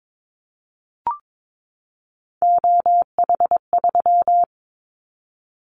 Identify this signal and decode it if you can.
Morse code, heard as a keyed tone, about 22 wpm, 700 Hz. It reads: OH3
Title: Callsign Prefixes for Top 25 DXCC - 22wpm